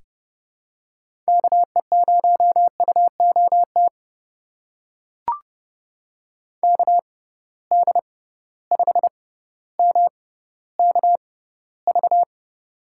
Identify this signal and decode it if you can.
Morse code, heard as a keyed tone, about 31 wpm, spaced as for 15 wpm, 700 Hz. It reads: KE0UOT KD5MKV
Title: USA Call Signs - 6 Characters - 6x Character Spacing - 30wpm